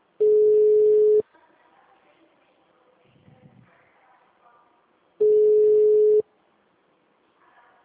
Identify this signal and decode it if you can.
Ringback tone (Europe, 425 Hz), with speech.